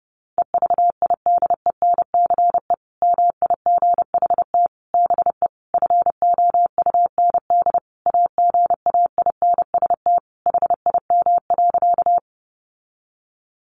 Morse code, heard as a keyed tone, about 28 wpm, 700 Hz. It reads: EVIDENCE MIGHT BE FOUND AGAINST HIM.
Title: Sentences from Top 1300 Words 30wpm